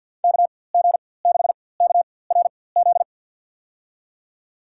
Morse code, heard as a keyed tone, about 52 wpm, spaced as for 23 wpm, 700 Hz. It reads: KG6KRZ
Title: USA Call Signs - 6 Characters - 4x Character Spacing - 50wpm